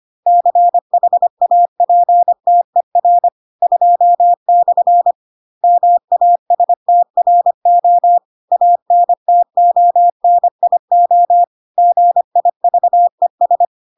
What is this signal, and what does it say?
Morse code, heard as a keyed tone, about 25 wpm, 700 Hz. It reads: CHAPTER 2/ MASTRO ANTONIO GIVES